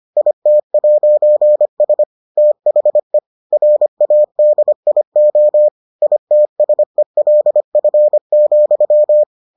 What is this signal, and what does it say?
Morse code, heard as a keyed tone, about 25 wpm, 600 Hz. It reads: IT'S THE RADIO ITSELF,